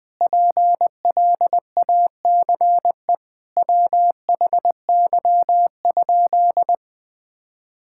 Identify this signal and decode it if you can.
Morse code, heard as a keyed tone, about 20 wpm, 700 Hz. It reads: PLACE WHY?
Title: Sentences with 4-Word Phrases - 20wpm